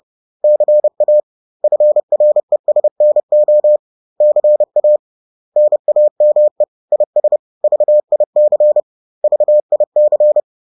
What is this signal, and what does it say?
Morse code, heard as a keyed tone, about 30 wpm, 600 Hz. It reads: CA FRESNO CA NAME IS VIC VIC